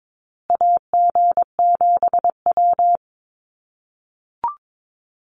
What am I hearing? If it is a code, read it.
Morse code, heard as a keyed tone, about 22 wpm, 700 Hz. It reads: AG7W
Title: USA Call Signs Repeated 3x 22wpm